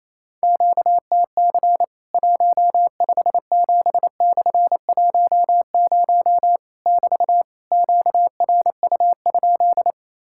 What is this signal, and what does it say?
Morse code, heard as a keyed tone, about 28 wpm, 700 Hz. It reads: QTC 157/10 <BT> QRU?
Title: Worked All Europe DX Contest - QTC Exchange - 28wpm